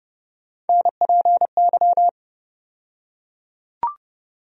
Morse code, heard as a keyed tone, about 31 wpm, 700 Hz. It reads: NPY
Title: USA Call Sign Postfixes - 3 Letters 30wpm